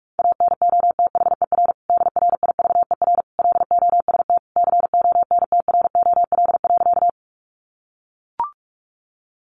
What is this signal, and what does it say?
Morse code, heard as a keyed tone, about 44 wpm, 700 Hz. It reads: ANOTHER DRIVER LOST CONTROL.
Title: Sentences from Top 1400 Words 45wpm